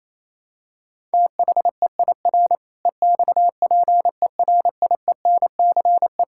Morse code, heard as a keyed tone, about 28 wpm, 700 Hz. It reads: THEIR EXPERIENCE
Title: Sentences from Top 1000 Words - Review Part 6 - 28wpm